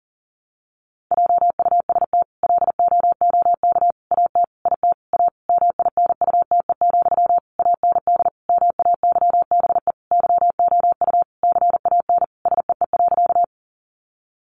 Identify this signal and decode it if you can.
Morse code, heard as a keyed tone, about 40 wpm, 700 Hz. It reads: JUST LOOK AT IT A MINUTE, AND MAYBE YOU CAN SEE.
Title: Sentences from Top 1000 Words 40wpm